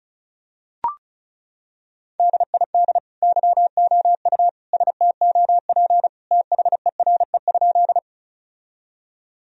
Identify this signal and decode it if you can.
Morse code, heard as a keyed tone, about 34 wpm, 700 Hz. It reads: DID YOU STOP THERE?
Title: Sentences from Top 1000 Words - Review Part 1 - 35wpm